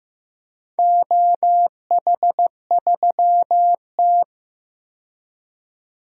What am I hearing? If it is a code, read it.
Morse code, heard as a keyed tone, about 15 wpm, 700 Hz. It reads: OH3T
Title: DXCC-100 Call Signs - 4 Characters - 15wpm